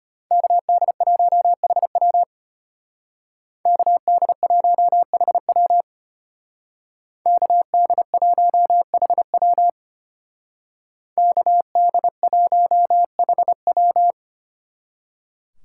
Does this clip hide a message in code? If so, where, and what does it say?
Yes: Morse code from 0.3 to 14.1 s, heard as a keyed tone, about 33 wpm, 700 Hz. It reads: KD1HW KD1HW KD1HW KD1HW